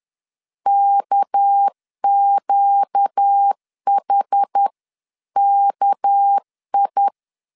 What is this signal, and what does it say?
Morse code, heard as a keyed tone, about 10 wpm, 800 Hz. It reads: KQHKI
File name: LEZIONI / 6 PKX0 / 6 PK 10wpm 2436